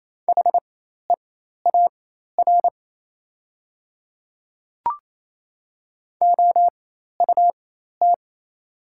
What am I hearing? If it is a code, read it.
Morse code, heard as a keyed tone, about 28 wpm, spaced as for 13 wpm, 700 Hz. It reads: HEAR OUT
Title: 4x Character Spacing - Top 300 Words - 28wpm